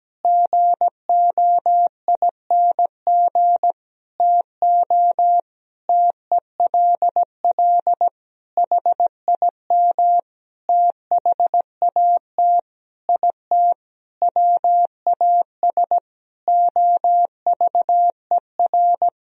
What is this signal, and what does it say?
Morse code, heard as a keyed tone, about 17 wpm, 700 Hz. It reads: GOING TO TELL HIM THAT IT WAS OVER
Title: Sentences from Top 1000 Words - Review Part 1 - 17wpm